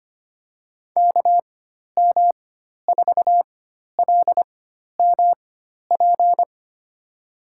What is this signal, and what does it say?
Morse code, heard as a keyed tone, about 25 wpm, spaced as for 12 wpm, 700 Hz. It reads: KM4LMP
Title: USA Call Signs - 6 Characters - 4x Character Spacing - 25wpm